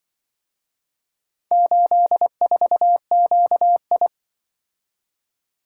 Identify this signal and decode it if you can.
Morse code, heard as a keyed tone, about 24 wpm, 700 Hz. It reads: 84QI
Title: Speed-Racing - Four Letter-Number Combinations - 20wpm